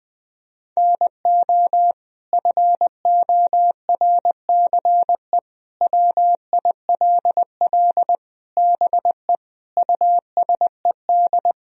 Morse code, heard as a keyed tone, about 20 wpm, 700 Hz. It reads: NO FORCE WILL BE USED
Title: Sentences from Top 1000 Words - Review Part 3 - 20wpm